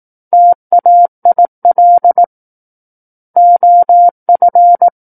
Morse code, heard as a keyed tone, about 18 wpm, 700 Hz. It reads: TAIL OF
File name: 18 WPM